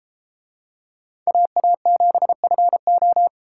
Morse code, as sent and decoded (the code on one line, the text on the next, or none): .- .- --... ..-. ---
AA7FO